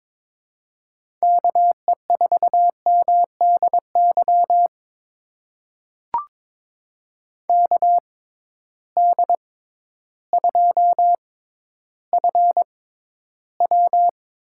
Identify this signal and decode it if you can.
Morse code, heard as a keyed tone, about 23 wpm, spaced as for 12 wpm, 700 Hz. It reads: KE4MDY KD2FW
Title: USA Call Signs - 6 Characters - 6x Character Spacing - 22wpm